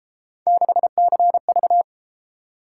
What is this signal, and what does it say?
Morse code, heard as a keyed tone, about 32 wpm, 700 Hz. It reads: 6CV